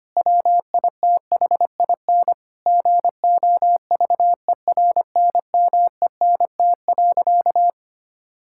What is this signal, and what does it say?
Morse code, heard as a keyed tone, about 25 wpm, 700 Hz. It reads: WITHIN GOVERNMENT.
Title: Sentences from Top 1000 Words - Review Part 5 - 25wpm